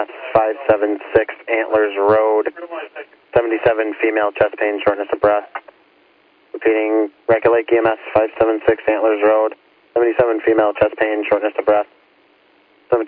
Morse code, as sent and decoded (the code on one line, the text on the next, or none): none